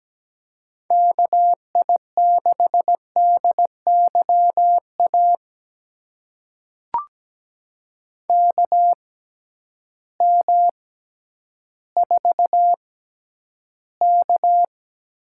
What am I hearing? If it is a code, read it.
Morse code, heard as a keyed tone, about 17 wpm, spaced as for 10 wpm, 700 Hz. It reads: KI6DYA KM4K
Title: USA Call Signs - 6 Characters - 6x Character Spacing - 17wpm